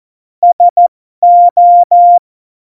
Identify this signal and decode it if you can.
Morse code, heard as a keyed tone, about 14 wpm, spaced as for 12 wpm, 700 Hz. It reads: SO
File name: Morsekurs / Witze